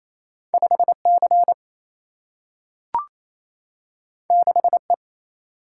Morse code, heard as a keyed tone, about 28 wpm, 700 Hz. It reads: 5C 6E
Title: Single-Letter Single-Number 28wpm